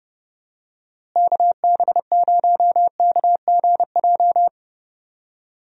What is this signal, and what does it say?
Morse code, heard as a keyed tone, about 32 wpm, 700 Hz. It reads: KB0KGJ
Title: USA Call Signs - 6 Characters - Repeated 3x - 30wpm